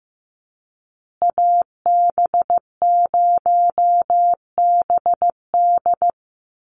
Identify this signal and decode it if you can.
Morse code, heard as a keyed tone, about 15 wpm, 700 Hz. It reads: AB0BD
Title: USA Call Signs Repeated 2x 15wpm